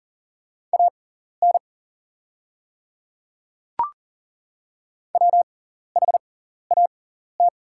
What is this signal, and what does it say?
Morse code, heard as a keyed tone, about 42 wpm, spaced as for 14 wpm, 700 Hz. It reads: AN WHAT
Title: 6x Character Spacing - Top 100 Words - 40wpm